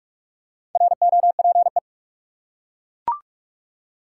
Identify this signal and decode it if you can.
Morse code, heard as a keyed tone, about 41 wpm, 700 Hz. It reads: ROPE
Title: Four Letter Words - 45wpm